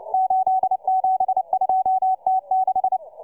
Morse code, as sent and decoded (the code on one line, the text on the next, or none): ---.. --... ..--- - -....
872T6